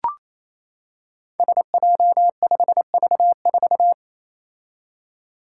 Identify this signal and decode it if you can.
Morse code, heard as a keyed tone, about 28 wpm, 700 Hz. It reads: SJ5V4
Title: Five Challenging Letter-Number Combinations 28wpm